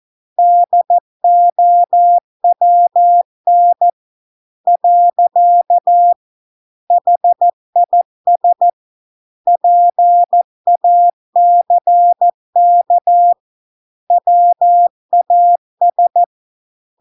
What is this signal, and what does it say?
Morse code, heard as a keyed tone, about 14 wpm, 700 Hz. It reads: DOWN . HIS PACK WAS